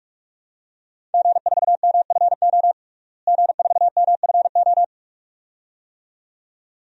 Morse code, heard as a keyed tone, about 46 wpm, 700 Hz. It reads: G4MFO G4MFO
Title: DXCC-100 Call Signs - 5 Characters - Repeated 2x - 45wpm